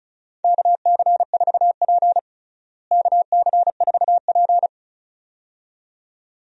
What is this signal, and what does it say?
Morse code, heard as a keyed tone, about 36 wpm, 700 Hz. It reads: KC4P KC4P
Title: USA Call Signs - 4 Characters - Repeated 2x - 35wpm